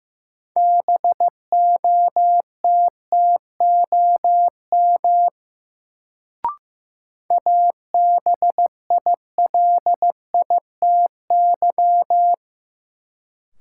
Morse code, heard as a keyed tone, about 15 wpm, 700 Hz. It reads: BOTTOM ABILITY
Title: Sets of 3 Words - New Word Drill - 15wpm